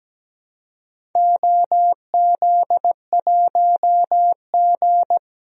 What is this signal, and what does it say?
Morse code, heard as a keyed tone, about 17 wpm, 700 Hz. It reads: OZ1G
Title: DXCC-100 Call Signs - 4 Characters - 17wpm